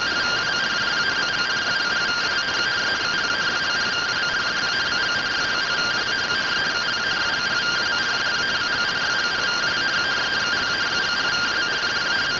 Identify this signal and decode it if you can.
Two-tone teleprinter signal, about 170 Hz shift.